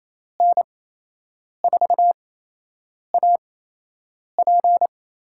Morse code, heard as a keyed tone, about 28 wpm, spaced as for 8 wpm, 700 Hz. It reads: N4AP
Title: USA Call Signs - 6 Characters - 8x Character Spacing - 28wpm